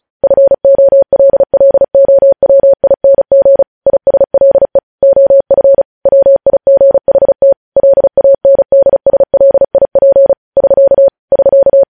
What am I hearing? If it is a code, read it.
Morse code, heard as a keyed tone, about 36 wpm, 550 Hz. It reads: FOLLOWING ISLE OF WIGHT LANDSLIP <SK> <SK>
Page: news headlines in morse code at 35 wpm